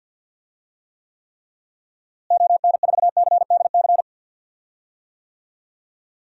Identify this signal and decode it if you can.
Morse code, heard as a keyed tone, about 53 wpm, 700 Hz. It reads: ON4CDC
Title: DXCC-100 Call Signs - 6 Characters - Repeated 3x - 50wpm